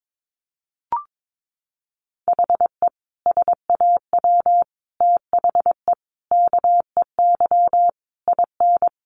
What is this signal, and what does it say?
Morse code, heard as a keyed tone, about 22 wpm, 700 Hz. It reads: HE SAW THE KEY IN
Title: Sentences from Top 1300 Words 22wpm